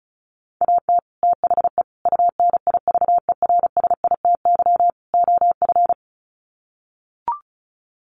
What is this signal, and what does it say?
Morse code, heard as a keyed tone, about 34 wpm, 700 Hz. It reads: AT THE UNIVERSITY OF
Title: Sets of 4 Words 35wpm